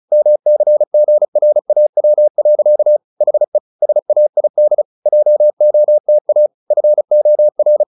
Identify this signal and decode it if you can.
Morse code, heard as a keyed tone, about 35 wpm, 600 Hz. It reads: MCGRAW. HE SAID JOTA FOR
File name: the old man and the sea / the old man and the sea 1 35wpm